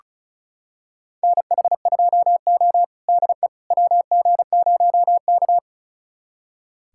Morse code, heard as a keyed tone, about 36 wpm, 700 Hz. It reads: NH2O DE WG0K
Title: USA Callsign DE Callsign - 4 Characters - 35wpm